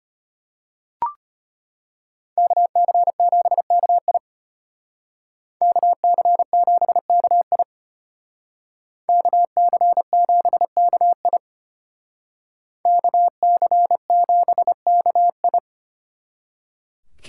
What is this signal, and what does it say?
Morse code, heard as a keyed tone, about 30 wpm, 700 Hz. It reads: KC7KI KC7KI KC7KI KC7KI
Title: Speed Racing - USA Call Signs 25wpm